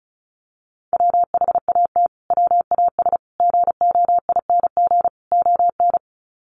Morse code, heard as a keyed tone, about 34 wpm, 700 Hz. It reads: WHAT WAS GOING ON
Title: Sets of 4 Words 35wpm